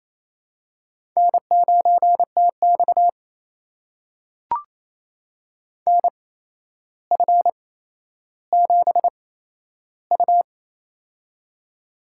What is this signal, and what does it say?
Morse code, heard as a keyed tone, about 28 wpm, spaced as for 12 wpm, 700 Hz. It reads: N9TX NF7U
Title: USA Call Signs - 4 Characters - 8x Character Spacing - 28wpm